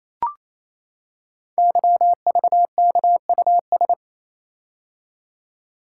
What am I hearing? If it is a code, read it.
Morse code, heard as a keyed tone, about 27 wpm, 700 Hz. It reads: YVKUS
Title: Code Groups - 5 Letters - 28wpm